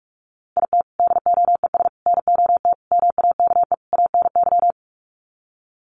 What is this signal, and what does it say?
Morse code, heard as a keyed tone, about 44 wpm, 700 Hz. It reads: IT DOES NOT MAKE ANY
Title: Sets of 5 Words Encore2 45wpm